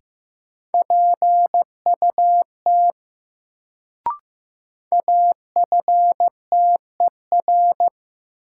Morse code, heard as a keyed tone, about 15 wpm, 700 Hz. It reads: PUT AFTER
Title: Sentences from Top 200 Words - New Word Drill - 15wpm